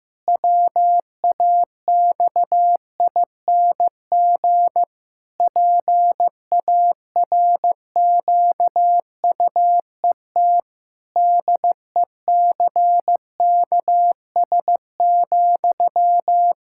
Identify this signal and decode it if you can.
Morse code, heard as a keyed tone, about 15 wpm, 700 Hz. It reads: WAXING PARQUET DECKS,
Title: Pangram Challenge v2 - 15wpm